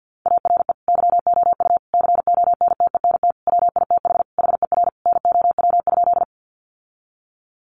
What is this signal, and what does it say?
Morse code, heard as a keyed tone, about 50 wpm, 700 Hz. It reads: ARE YOU CONTENT WITH HER NOW?
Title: Sentences from Top 900 Words 50wpm